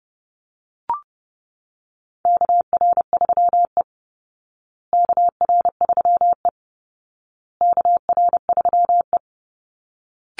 Morse code, heard as a keyed tone, about 28 wpm, 700 Hz. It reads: KR3E KR3E KR3E
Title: Call Signs Repeated 2x - 30wpm